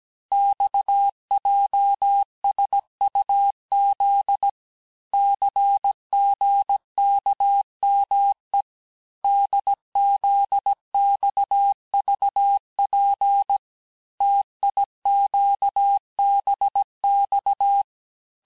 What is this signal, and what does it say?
Morse code, heard as a keyed tone, about 17 wpm, 800 Hz. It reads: XJSUZ CGKME DZXVP TIQBX
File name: corrige-cours-1581-texte-chiffre-vitesse-16-mpm